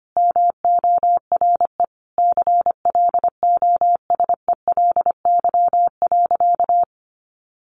Morse code, heard as a keyed tone, about 25 wpm, 700 Hz. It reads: MORE CLOSELY.
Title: Sentences from Top 1400 Words 25wpm